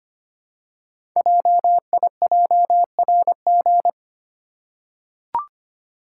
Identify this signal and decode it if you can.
Morse code, heard as a keyed tone, about 25 wpm, 700 Hz. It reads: JIJRG
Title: Code Groups - 5 Letters - 25wpm